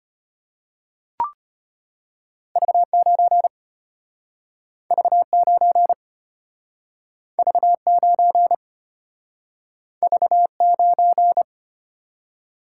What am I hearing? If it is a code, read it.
Morse code, heard as a keyed tone, about 34 wpm, 700 Hz. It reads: V9 V9 V9 V9